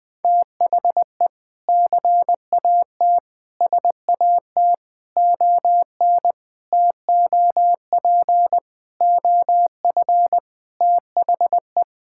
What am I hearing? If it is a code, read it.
Morse code, heard as a keyed tone, about 19 wpm, 700 Hz. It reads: THE CAT SAT ON TOP OF THE
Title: Sentences with 4-Word Phrases - Spoken Beginning - 20wpm